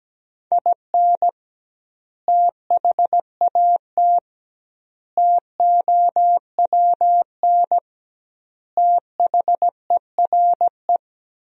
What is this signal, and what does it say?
Morse code, heard as a keyed tone, about 17 wpm, 700 Hz. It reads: IN THAT TOWN THERE
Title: 2x Word-Spacing - Sentences from Top 500 Words - 17wpm